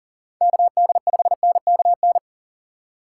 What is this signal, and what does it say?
Morse code, heard as a keyed tone, about 40 wpm, 700 Hz. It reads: KD5NKN
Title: USA Call Signs - 6 Characters - 2x Character Spacing - 40wpm